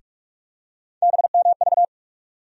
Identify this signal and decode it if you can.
Morse code, heard as a keyed tone, about 48 wpm, 700 Hz. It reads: BMV
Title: USA Call Sign Postfixes - 3 Letters 45wpm